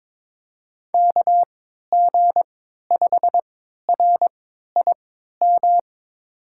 Morse code, heard as a keyed tone, about 22 wpm, spaced as for 12 wpm, 700 Hz. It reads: KG5RIM
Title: USA Call Signs - 3x Character Spacing - 22wpm